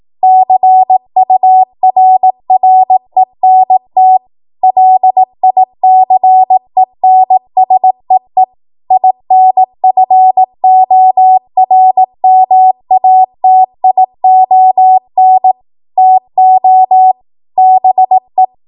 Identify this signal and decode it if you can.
Morse code, heard as a keyed tone, about 18 wpm, 750 Hz. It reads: CURRENT LICENSEE INFORMATION TO BE
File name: Morse / 18WPM